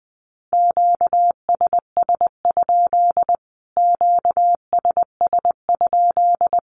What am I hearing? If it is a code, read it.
Morse code, heard as a keyed tone, about 20 wpm, 700 Hz. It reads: QSS? QSS?
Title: Q-Codes Repeated 3x 20wpm